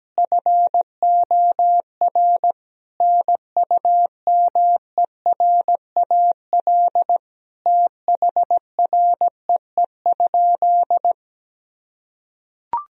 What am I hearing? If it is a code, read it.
Morse code, heard as a keyed tone, about 17 wpm, 700 Hz. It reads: FOR NUMERAL THREE?
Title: Sentences from Top 1000 Words - Review Part 2 - 17wpm